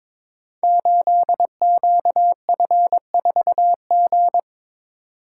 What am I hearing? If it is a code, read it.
Morse code, heard as a keyed tone, about 22 wpm, 700 Hz. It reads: 8QF4G
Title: Code Groups - 5 Letter:Numbers - 22wpm